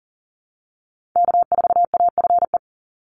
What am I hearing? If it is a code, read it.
Morse code, heard as a keyed tone, about 40 wpm, 700 Hz. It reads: K4AFE
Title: Call Signs Repeated 2x - 40wpm